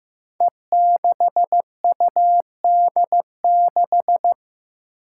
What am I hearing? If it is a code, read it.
Morse code, heard as a keyed tone, about 15 wpm, 700 Hz. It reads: E6UD6
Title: Five Challenging Letter-Number Combinations 15wpm